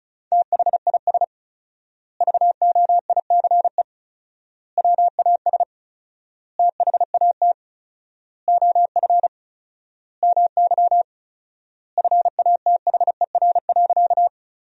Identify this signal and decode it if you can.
Morse code, heard as a keyed tone, about 34 wpm, 700 Hz. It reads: THIS VOICE WAS THAT OF MY FATHER.
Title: 4x Word-Spacing - Sentences from Top 500 Words - 35wpm